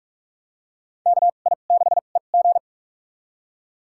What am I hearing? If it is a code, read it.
Morse code, heard as a keyed tone, about 46 wpm, spaced as for 32 wpm, 700 Hz. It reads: KI6EG
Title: USA Call Signs - 2x Character Spacing - 45wpm